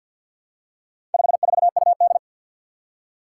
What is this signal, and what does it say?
Morse code, heard as a keyed tone, about 53 wpm, 700 Hz. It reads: 54UD
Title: Four Challenging Letter-Number Combinations 50wpm